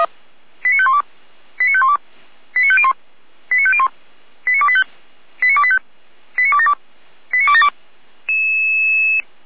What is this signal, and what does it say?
Five-tone selective call (ZVEI), heard as quick runs of five tones, tones of about 70 ms. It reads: 89512 89512 89501 89501 89285 89285 89272 89272
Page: ZVEI